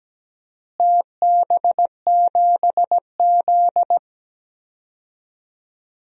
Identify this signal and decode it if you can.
Morse code, heard as a keyed tone, about 17 wpm, 700 Hz. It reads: TB7Z
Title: DXCC-100 Call Signs - 4 Characters - Repeated 2x - 17wpm